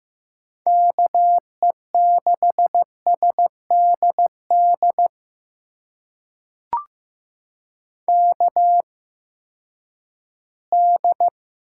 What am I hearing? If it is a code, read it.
Morse code, heard as a keyed tone, about 15 wpm, spaced as for 10 wpm, 700 Hz. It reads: KE6SDD KD
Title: USA Call Signs - 6 Characters - 8x Character Spacing - 15wpm